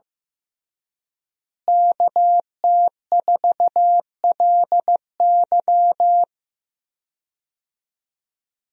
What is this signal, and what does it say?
Morse code, heard as a keyed tone, about 15 wpm, 700 Hz. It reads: KT4LY
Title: USA Call Signs - 5 Characters - Repeated 2x - 15wpm